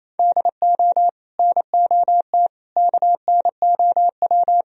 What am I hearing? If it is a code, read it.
Morse code, heard as a keyed tone, about 37 wpm, 700 Hz. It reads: DO NOT KNOW